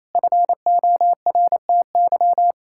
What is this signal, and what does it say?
Morse code, heard as a keyed tone, about 28 wpm, 700 Hz. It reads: FORTY
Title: Pangram Challenge v2 - 28wpm